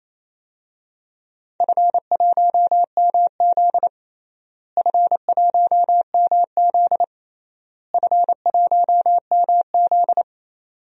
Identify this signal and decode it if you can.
Morse code, heard as a keyed tone, about 28 wpm, 700 Hz. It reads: F1MZ F1MZ F1MZ
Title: DXCC-100 Call Signs - 4 Characters - Repeated 3x - 28wpm